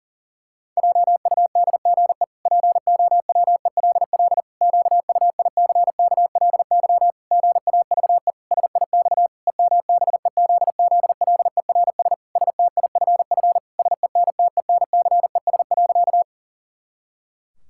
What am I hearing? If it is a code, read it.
Morse code, heard as a keyed tone, about 40 wpm, 700 Hz. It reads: JUDGE POWELL QUICKLY GAVE SIX EMBEZZLERS STIFF SENTENCES.
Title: Pangram Challenge v2 - 40wpm